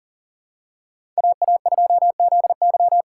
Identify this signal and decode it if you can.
Morse code, heard as a keyed tone, about 37 wpm, 700 Hz. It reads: AA2ZY
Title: USA Call Signs - 5 Characters - 2x Character Spacing - 40wpm